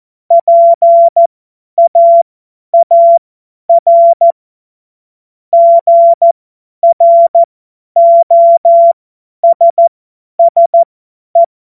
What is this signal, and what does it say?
Morse code, heard as a keyed tone, about 14 wpm, spaced as for 10 wpm, 680 Hz. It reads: PAAR GROSSE
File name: Morsekurs / Witze / Witz 02. Dezember 2025 50 F70